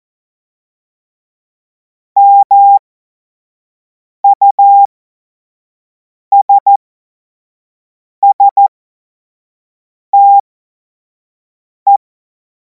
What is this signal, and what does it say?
Morse code, heard as a keyed tone, about 14 wpm, spaced as for 5 wpm, 800 Hz. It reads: MUSSTE
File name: Morsekurs / Witze / Witz 12. Mai 2026 25 F70